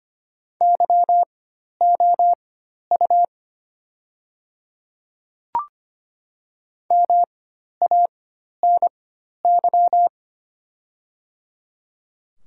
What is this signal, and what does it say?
Morse code, heard as a keyed tone, about 25 wpm, spaced as for 12 wpm, 700 Hz. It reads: YOU MANY
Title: 4x Character Spacing - Top 300 Words - 25wpm